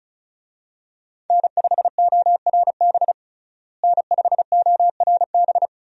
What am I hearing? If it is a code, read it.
Morse code, heard as a keyed tone, about 36 wpm, 700 Hz. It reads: N5ORB N5ORB
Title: USA Call Signs - 5 Characters - Repeated 2x - 35wpm